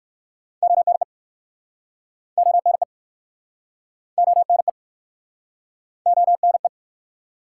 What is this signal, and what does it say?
Morse code, heard as a keyed tone, about 55 wpm, 700 Hz. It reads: ONE ONE ONE ONE
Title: Speed-Racing Top 180 QSO Elements 45wpm